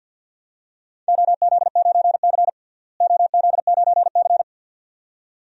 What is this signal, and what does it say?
Morse code, heard as a keyed tone, about 47 wpm, 700 Hz. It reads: OZ9C OZ9C
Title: DXCC-100 Call Signs - 4 Characters - Repeated 2x - 50wpm